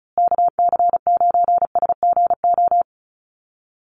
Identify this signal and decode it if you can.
Morse code, heard as a keyed tone, about 34 wpm, 700 Hz. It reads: KC9SGO
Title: USA Call Signs Repeated 3x 35wpm